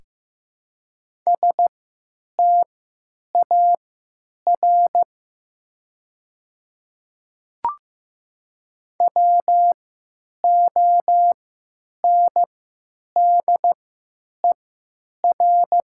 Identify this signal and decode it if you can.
Morse code, heard as a keyed tone, about 15 wpm, spaced as for 9 wpm, 700 Hz. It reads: STAR WONDER
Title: 3x Character Spacing - Top 500 Words - 15wpm